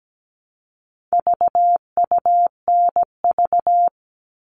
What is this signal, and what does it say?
Morse code, heard as a keyed tone, about 17 wpm, 700 Hz. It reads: VUNV
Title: All Cut Numbers - 4-Digit 17wpm